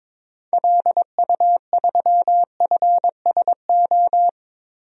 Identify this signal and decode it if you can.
Morse code, heard as a keyed tone, about 22 wpm, 700 Hz. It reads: LU3FSO
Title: DXCC-100 Call Signs - 6 Characters - Repeated 2x - 22wpm